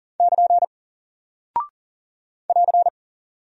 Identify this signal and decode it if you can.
Morse code, heard as a keyed tone, about 38 wpm, 700 Hz. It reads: <KN> <AR>
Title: Single Character (Letter:Number:Punctuation:Prosign) Alt 40wpm